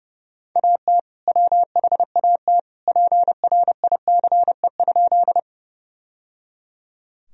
Morse code, heard as a keyed tone, about 30 wpm, 700 Hz. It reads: AT WHAT PRICE?
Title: Sentences from Top 1000 Words - Review Part 4 - 30wpm